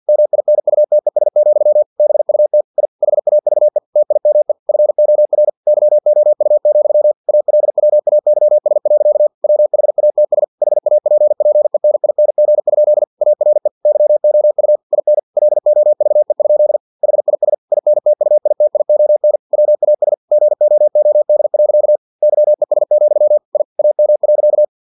Morse code, heard as a keyed tone, about 50 wpm, 600 Hz. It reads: MINUTES, BUT I HAVE TIME FOR YOU, ALWAYS. WHATS HAPPENING? ARE YOU IN LOVE? HIS INTUITION WAS GOOD. YES, I AM.
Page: Never 49wpm